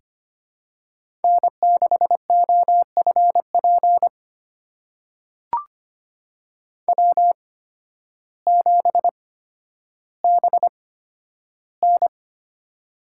Morse code, heard as a keyed tone, about 25 wpm, spaced as for 12 wpm, 700 Hz. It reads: N6OFP W7BN
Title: USA Call Signs - 8x Character Spacing - 25wpm